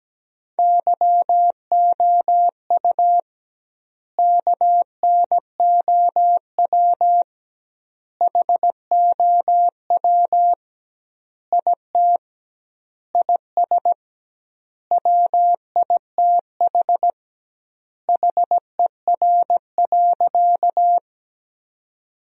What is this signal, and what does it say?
Morse code, heard as a keyed tone, about 17 wpm, 700 Hz. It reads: YOU KNOW HOW IT IS WITH HER.
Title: 2x Word-Spacing - Sentences from Top 100 Words - 17wpm